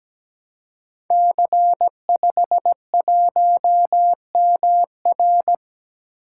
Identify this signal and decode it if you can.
Morse code, heard as a keyed tone, about 17 wpm, 700 Hz. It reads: C51MR
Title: Code Groups - 5 Letter:Numbers - 17wpm